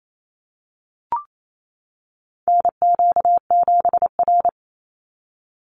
Morse code, heard as a keyed tone, about 28 wpm, 700 Hz. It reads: NQ7R
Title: NA SOTA Chaser Callsigns 28wpm